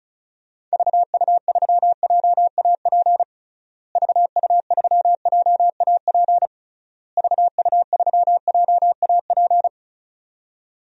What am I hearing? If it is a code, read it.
Morse code, heard as a keyed tone, about 36 wpm, 700 Hz. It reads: VU3JAP VU3JAP VU3JAP
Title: DXCC-100 Call Signs - 6 Characters - Repeated 3x - 35wpm